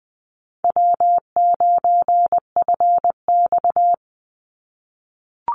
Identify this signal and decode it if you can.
Morse code, heard as a keyed tone, about 20 wpm, 700 Hz. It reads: W9FX
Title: Easy Call Signs - 20wpm